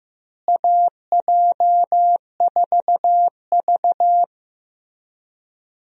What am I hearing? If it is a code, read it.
Morse code, heard as a keyed tone, about 15 wpm, 700 Hz. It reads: AJ4V
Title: USA Call Signs - 4 Characters - Repeated 3x - 15wpm